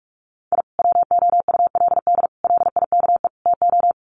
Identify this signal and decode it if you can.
Morse code, heard as a keyed tone, about 44 wpm, 700 Hz. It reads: I WOULD LIKE TO
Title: Sets of 4 Words 45wpm